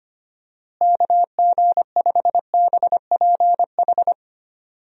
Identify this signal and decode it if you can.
Morse code, heard as a keyed tone, about 25 wpm, 700 Hz. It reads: KG5BPH
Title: USA Call Signs - 6 Characters - Repeated 3x - 25wpm